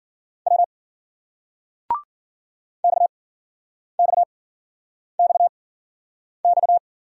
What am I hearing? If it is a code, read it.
Morse code, heard as a keyed tone, about 56 wpm, spaced as for 9 wpm, 700 Hz. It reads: W XXXX